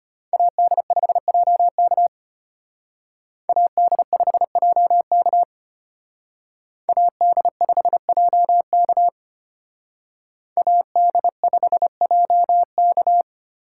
Morse code, heard as a keyed tone, about 32 wpm, 700 Hz. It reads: AD5JK AD5JK AD5JK AD5JK